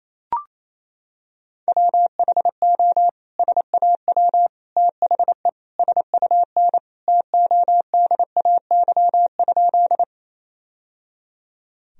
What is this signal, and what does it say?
Morse code, heard as a keyed tone, about 27 wpm, 700 Hz. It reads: WHO SAW THE SUN TODAY?
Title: Sentences from Top 1000 Words - Review Part 1 - 28wpm